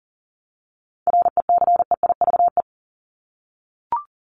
Morse code, heard as a keyed tone, about 40 wpm, 700 Hz. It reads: RECEIVE
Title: Top 751-1000 Words 40wpm